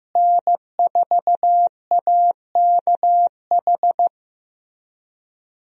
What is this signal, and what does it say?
Morse code, heard as a keyed tone, about 15 wpm, 700 Hz. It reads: N4AKH
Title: USA Call Signs - 5 Characters - Repeated 3x - 15wpm